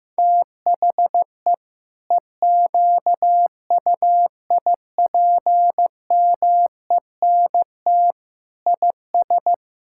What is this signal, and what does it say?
Morse code, heard as a keyed tone, about 15 wpm, 700 Hz. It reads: THE EQUIPMENT IS
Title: Sentences from Top 1000 Words - Review Part 5 - 15wpm